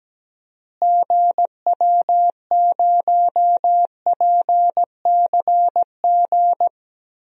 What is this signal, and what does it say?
Morse code, heard as a keyed tone, about 17 wpm, 700 Hz. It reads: GW0PCG
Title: DXCC-100 Call Signs - 6 Characters - Repeated 3x - 17wpm